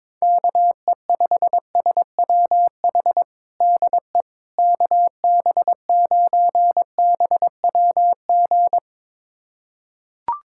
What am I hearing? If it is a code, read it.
Morse code, heard as a keyed tone, about 22 wpm, 700 Hz. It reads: KE5SWH DE KB9BWG
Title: USA Callsign DE Callsign - 6 Characters - 22wpm